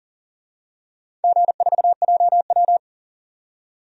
Morse code, heard as a keyed tone, about 40 wpm, 700 Hz. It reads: G4JW
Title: DXCC-100 Call Signs - 4 Characters - 40wpm